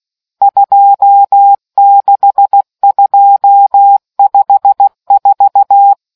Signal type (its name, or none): Morse code